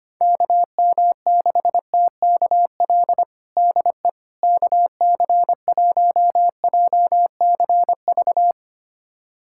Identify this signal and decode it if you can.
Morse code, heard as a keyed tone, about 25 wpm, 700 Hz. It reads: KM6TKL DE KC1JCV
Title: USA Callsign DE Callsign 25wpm